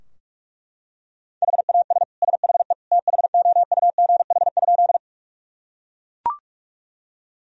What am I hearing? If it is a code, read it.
Morse code, heard as a keyed tone, about 41 wpm, 700 Hz. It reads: HAS SHE THOUGH?
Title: Sentences from Top 1000 Words - Review Part 2 - 45wpm